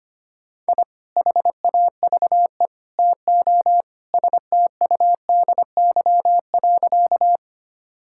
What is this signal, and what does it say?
Morse code, heard as a keyed tone, about 25 wpm, 700 Hz. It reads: I HAVE TO STUDY.
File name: practice / 25wpm / Sentences with 3-Word Phrases - 25wpm